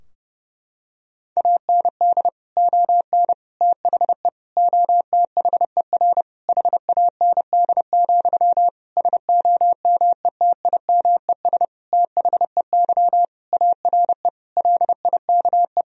Morse code, heard as a keyed tone, about 30 wpm, 700 Hz. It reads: AND ON THE OTHER HAND, SOMETIMES THEY ARE LIKE
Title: Sentences from Top 1000 Words - Review Part 3 - 30wpm